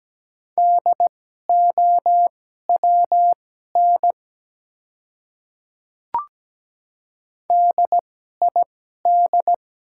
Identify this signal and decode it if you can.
Morse code, heard as a keyed tone, about 17 wpm, spaced as for 12 wpm, 700 Hz. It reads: DOWN DID
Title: 2x Character Spacing - Top 200 Words - 17wpm